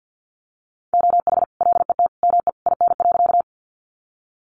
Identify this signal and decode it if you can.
Morse code, heard as a keyed tone, about 50 wpm, 700 Hz. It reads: OH LET ME IN.
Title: Sentences from Top 500 Words 50wpm